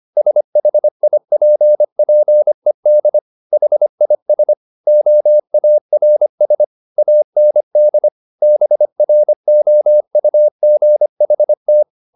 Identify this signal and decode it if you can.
Morse code, heard as a keyed tone, about 25 wpm, 600 Hz. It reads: SHIPPED HIS OARS AND BROUGHT
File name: the old man and the sea part 1 25wpm